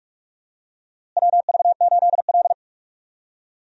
Morse code, heard as a keyed tone, about 46 wpm, 700 Hz. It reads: WV8L
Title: USA Call Signs - 4 Characters - Repeated 2x - 45wpm